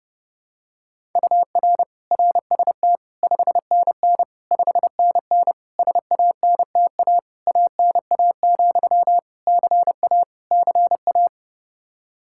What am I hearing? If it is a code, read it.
Morse code, heard as a keyed tone, about 32 wpm, 700 Hz. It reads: UR RST 5NN 5NN SANTA ANA, CA CA
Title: Practice QSO - Exchange 1 - Beginning - 30wpm